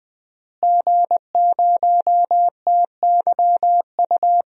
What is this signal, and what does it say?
Morse code, heard as a keyed tone, about 21 wpm, 700 Hz. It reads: G0TYU
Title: DXCC-100 Call Signs - 5 Characters - Repeated 2x - 20wpm